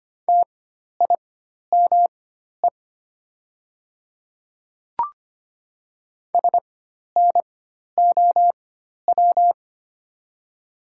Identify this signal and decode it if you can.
Morse code, heard as a keyed tone, about 25 wpm, spaced as for 12 wpm, 700 Hz. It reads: TIME SNOW
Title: 4x Character Spacing - Top 500 Words - 25wpm